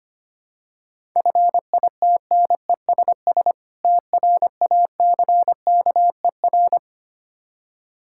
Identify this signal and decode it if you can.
Morse code, heard as a keyed tone, about 25 wpm, 700 Hz. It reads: FITNESS TRACKER
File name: practice / 25wpm / Difficult Words with Hints - 25wpm